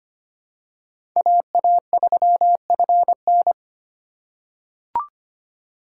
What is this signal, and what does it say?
Morse code, heard as a keyed tone, about 25 wpm, 700 Hz. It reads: AA3FN
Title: USA Call Signs - 5 Characters - 3x Character Spacing - 25wpm